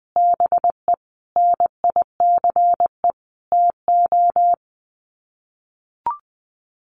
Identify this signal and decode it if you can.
Morse code, heard as a keyed tone, about 20 wpm, 700 Hz. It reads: BE NICE TO
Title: Sets of 5 Words Encore2 20wpm